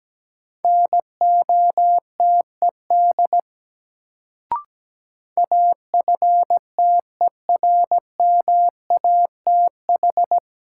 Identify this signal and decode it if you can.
Morse code, heard as a keyed tone, about 17 wpm, 700 Hz. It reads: NOTED AFTERMATH
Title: Sets of 5 Words - New Word Drill - 17wpm